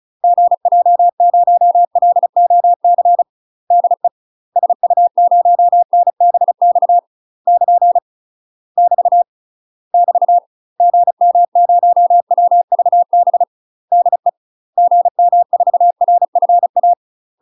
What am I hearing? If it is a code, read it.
Morse code, heard as a keyed tone, about 36 wpm, 700 Hz. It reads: GJ0LOC DE SU0NBX <KN> <BT> <BT> GM0WVB DE GM4RFA
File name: UK MASTER QSOs TRAINING/35WPM